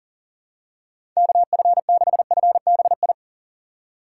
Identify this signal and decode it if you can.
Morse code, heard as a keyed tone, about 40 wpm, 700 Hz. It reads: KF6FBI
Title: USA Call Signs - 6 Characters - 4x Character Spacing - 40wpm